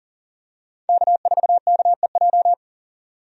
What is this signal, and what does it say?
Morse code, heard as a keyed tone, about 39 wpm, 700 Hz. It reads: K4KEJ
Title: USA Call Signs - 5 Characters - 40wpm